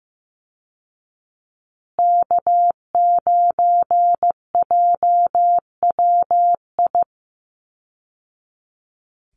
Morse code, heard as a keyed tone, about 15 wpm, 700 Hz. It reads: K9JWI
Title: Call Signs Repeated 2x - 15wpm